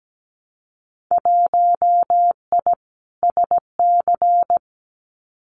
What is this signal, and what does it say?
Morse code, heard as a keyed tone, about 17 wpm, 700 Hz. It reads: 1I SC
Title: Winter Field Day Exchanges Encore 17wpm